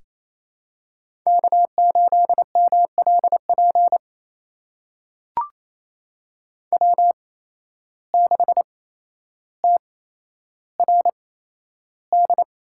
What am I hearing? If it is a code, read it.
Morse code, heard as a keyed tone, about 28 wpm, spaced as for 12 wpm, 700 Hz. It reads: K8MLP W6TRD
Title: USA Call Signs - 5 Characters - 8x Character Spacing - 28wpm